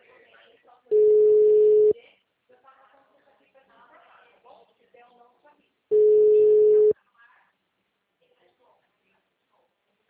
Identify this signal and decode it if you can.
Ringback tone (Europe, 425 Hz), with speech.